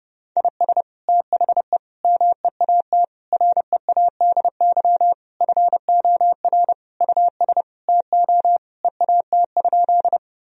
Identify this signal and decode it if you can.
Morse code, heard as a keyed tone, about 30 wpm, 700 Hz. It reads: IS THE MEAT READY FOR US TO EAT?
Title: Sentences from Top 1000 Words - Review Part 3 - 30wpm